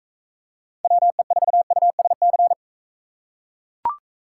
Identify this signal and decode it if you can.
Morse code, heard as a keyed tone, about 42 wpm, 700 Hz. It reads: WE4USC